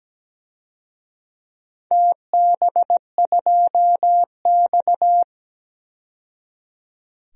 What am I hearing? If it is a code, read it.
Morse code, heard as a keyed tone, about 17 wpm, 700 Hz. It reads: TB2X
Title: DXCC-100 Call Signs - 4 Characters - 17wpm